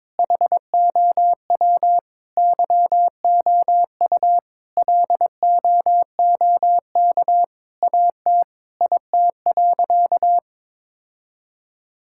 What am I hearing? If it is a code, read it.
Morse code, heard as a keyed tone, about 22 wpm, 700 Hz. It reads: HOW YOU LOOK AT IT.
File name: Sentences with 7-Word Phrases - 22wpm